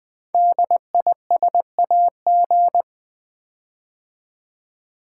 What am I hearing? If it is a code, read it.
Morse code, heard as a keyed tone, about 20 wpm, 700 Hz. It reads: DISAG
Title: Build A Word - 5 Letters At A Time - 12-Letter Words - 20wpm